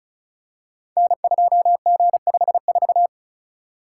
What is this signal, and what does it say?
Morse code, heard as a keyed tone, about 34 wpm, 700 Hz. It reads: N2G54
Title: Code Groups - 5 Letter:Numbers - 35wpm